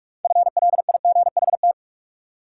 Morse code, heard as a keyed tone, about 44 wpm, 700 Hz. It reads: FLIGHT
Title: Sentences from Top 1000 Words - New Word Drill - 45wpm